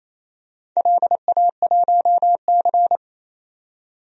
Morse code, heard as a keyed tone, about 28 wpm, 700 Hz. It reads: LA1C
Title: DXCC-100 Call Signs - 4 Characters - 28wpm